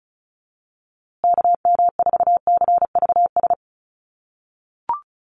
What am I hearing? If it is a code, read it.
Morse code, heard as a keyed tone, about 34 wpm, 700 Hz. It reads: KM4CVS
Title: USA Call Signs Repeated 3x 35wpm